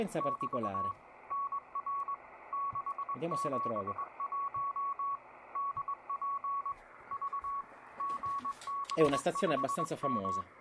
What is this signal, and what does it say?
Morse code, heard as a keyed tone, about 20 wpm, 1120 Hz, with speech and radio noise. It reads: <BT> NR 671 DP V GB5M